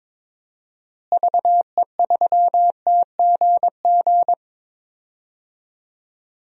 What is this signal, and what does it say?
Morse code, heard as a keyed tone, about 22 wpm, 700 Hz. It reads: VE3TGG
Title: DXCC-100 Call Signs - 6 Characters - Repeated 3x - 22wpm